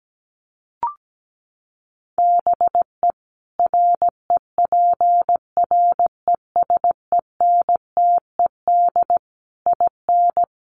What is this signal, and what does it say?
Morse code, heard as a keyed tone, about 17 wpm, 700 Hz. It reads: BE REPRESENTED IN
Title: Sets of 5 Words 17wpm